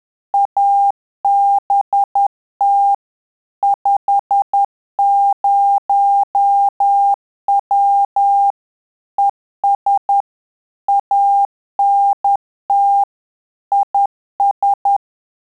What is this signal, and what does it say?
Morse code, heard as a keyed tone, about 10 wpm, 800 Hz. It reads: ABT 50W ES ANT IS
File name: Telegraphy-group / QSO CW 50 CPM